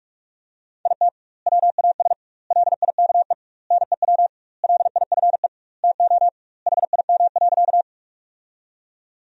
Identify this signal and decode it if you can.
Morse code, heard as a keyed tone, about 47 wpm, 700 Hz. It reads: IT WAS LIKE NEW LIFE TO HIM.
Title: 2x Word-Spacing - Sentences from Top 300 Words - 45wpm